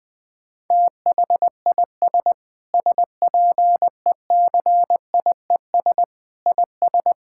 Morse code, heard as a keyed tone, about 20 wpm, 700 Hz. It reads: THIS SPECIES IS
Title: Sentences with 6-Word Phrases - 20wpm